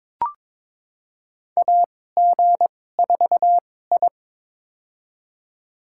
Morse code, heard as a keyed tone, about 22 wpm, spaced as for 16 wpm, 700 Hz. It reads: AG4I
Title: USA Call Signs - 2x Character Spacing - 22wpm